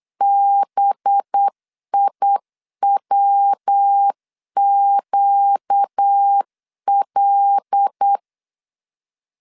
Morse code, heard as a keyed tone, about 8 wpm, 800 Hz. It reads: BIWQL
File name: LEZIONI / 7 GF9Z / 7 GF 8wpm 3108